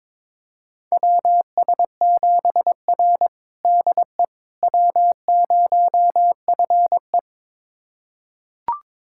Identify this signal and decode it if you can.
Morse code, heard as a keyed tone, about 22 wpm, 700 Hz. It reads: WS7R DE W0FE
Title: USA Callsign DE Callsign - 4 Characters - 22wpm